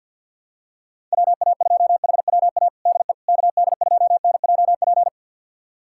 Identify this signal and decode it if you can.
Morse code, heard as a keyed tone, about 49 wpm, 700 Hz. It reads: WA2HWA DE KD2NJP
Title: USA Callsign DE Callsign - 6 Characters - 50wpm